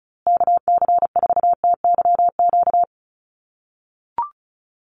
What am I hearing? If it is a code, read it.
Morse code, heard as a keyed tone, about 34 wpm, 700 Hz. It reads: KC4TYQ
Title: USA Call Signs Repeated 3x 35wpm